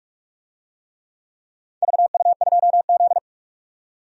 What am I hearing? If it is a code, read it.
Morse code, heard as a keyed tone, about 44 wpm, 690 Hz. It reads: VU2Z
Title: DXCC-100 Call Signs - 4 Characters - 45wpm